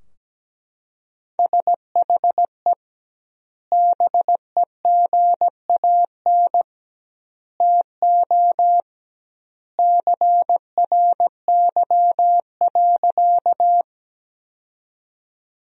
Morse code, heard as a keyed tone, about 17 wpm, 700 Hz. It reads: SHE BEGAN TO CRY.
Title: 2x Word-Spacing - Sentences from Top 500 Words - 17wpm